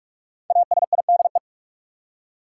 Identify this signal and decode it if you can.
Morse code, heard as a keyed tone, about 46 wpm, 700 Hz. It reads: ASIDE
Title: Five Letter Words - 45wpm